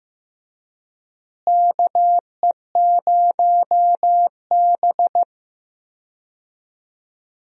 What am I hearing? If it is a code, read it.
Morse code, heard as a keyed tone, about 15 wpm, 700 Hz. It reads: KE0B
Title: USA Call Signs - 4 Characters - Repeated 3x - 15wpm